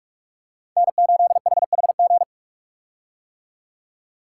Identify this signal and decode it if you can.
Morse code, heard as a keyed tone, about 41 wpm, 700 Hz. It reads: N8HHG
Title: K1USN SST QSO Practice - 45wpm